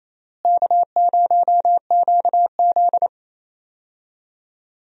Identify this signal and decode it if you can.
Morse code, heard as a keyed tone, about 28 wpm, 700 Hz. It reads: K0QZ
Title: USA Call Signs - 4 Characters - Repeated 3x - 28wpm